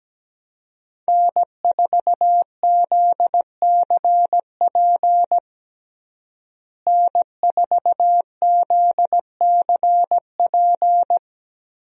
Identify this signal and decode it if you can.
Morse code, heard as a keyed tone, about 17 wpm, 700 Hz. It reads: N4ZCP N4ZCP
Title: USA Call Signs - 5 Characters - Repeated 2x - 17wpm